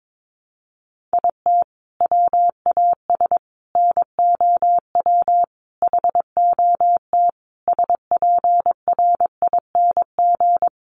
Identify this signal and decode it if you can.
Morse code, heard as a keyed tone, about 22 wpm, 700 Hz. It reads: IT WAS NOW HOT SPRING
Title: Sentences from Top 1200 Words 22wpm